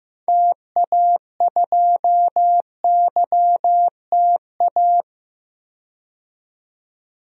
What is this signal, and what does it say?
Morse code, heard as a keyed tone, about 15 wpm, 700 Hz. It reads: TA2YTA
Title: DXCC-100 Call Signs - 6 Characters - 15wpm